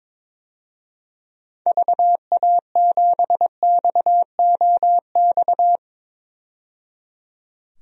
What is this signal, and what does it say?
Morse code, heard as a keyed tone, about 21 wpm, 700 Hz. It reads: VA7XOX
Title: DXCC-100 Call Signs - 6 Characters - 22wpm